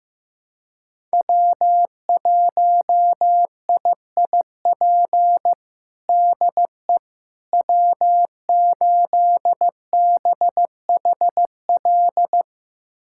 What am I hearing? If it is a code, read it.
Morse code, heard as a keyed tone, about 15 wpm, 700 Hz. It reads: W1IIP DE W8BHL
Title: USA Callsign DE Callsign - 5 Characters - 15wpm